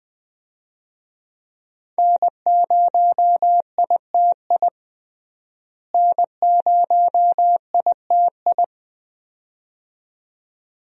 Morse code, heard as a keyed tone, about 20 wpm, 700 Hz. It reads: N0ITI N0ITI
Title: USA Call Signs - 5 Characters - Repeated 2x - 20wpm